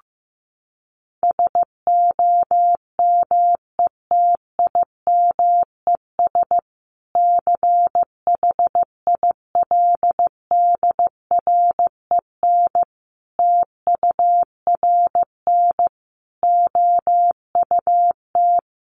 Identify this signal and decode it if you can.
Morse code, heard as a keyed tone, about 15 wpm, 700 Hz. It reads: SOMETIMES CHILDREN TURN OUT
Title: Sentences from Top 600 Words 15wpm